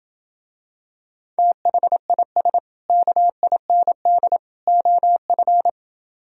Morse code, heard as a keyed tone, about 27 wpm, 700 Hz. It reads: THIS KIND OF